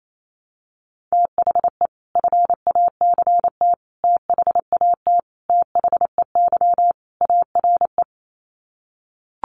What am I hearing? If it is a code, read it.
Morse code, heard as a keyed tone, about 28 wpm, 700 Hz. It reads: THE FACT THAT THEY ARE
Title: Sets of 5 Words Encore2 28wpm